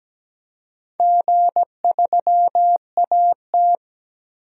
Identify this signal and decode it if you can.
Morse code, heard as a keyed tone, about 17 wpm, 700 Hz. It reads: G3AT
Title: DXCC-100 Call Signs - 4 Characters - 17wpm